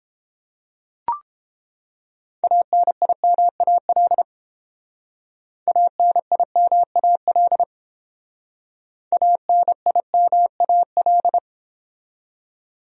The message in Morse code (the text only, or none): ANIMAL ANIMAL ANIMAL